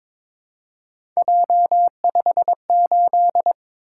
Morse code, heard as a keyed tone, about 22 wpm, 700 Hz. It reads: J58
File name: practice / 22wpm / Three Letter-Number Combinations - 22wpm